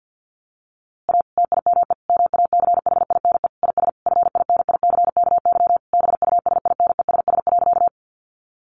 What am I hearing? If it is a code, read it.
Morse code, heard as a keyed tone, about 50 wpm, 700 Hz. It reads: A TIME MACHINE IS FINICKY BUSINESS.
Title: A Sound Of Thunder 50wpm